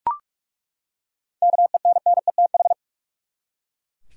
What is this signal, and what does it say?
Morse code, heard as a keyed tone, about 42 wpm, 700 Hz. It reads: KENNETH
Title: USA Common Names - 45wpm